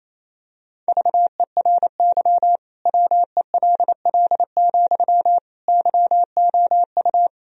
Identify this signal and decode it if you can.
Morse code, heard as a keyed tone, about 28 wpm, 700 Hz. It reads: VERY WELL, YOU
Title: Sentences from Top 1000 Words - Review Part 4 - 28wpm